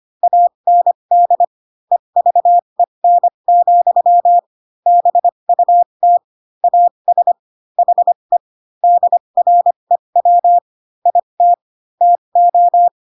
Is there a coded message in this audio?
Yes: Morse code from 0.2 to 12.9 s, heard as a keyed tone, about 25 wpm, spaced as for 22 wpm, 700 Hz. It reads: AND EVEN, BUT AS HE DREW IT TO